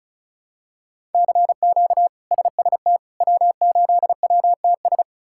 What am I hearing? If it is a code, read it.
Morse code, heard as a keyed tone, about 36 wpm, 700 Hz. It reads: CQ SST W8WTS
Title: K1USN SST QSO Practice - 35wpm